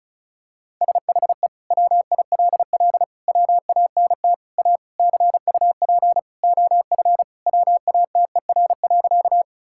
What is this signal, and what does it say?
Morse code, heard as a keyed tone, about 36 wpm, 700 Hz. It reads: SHE WILL WANT A CUP OF WATER.
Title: Sentences from Top 1600 Words 35wpm